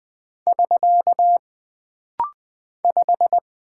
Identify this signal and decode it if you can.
Morse code, heard as a keyed tone, about 20 wpm, 700 Hz. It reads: <SK> 5
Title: Single Character (Letter:Number:Punctuation:Prosign) - Rapid-Fire - 20wpm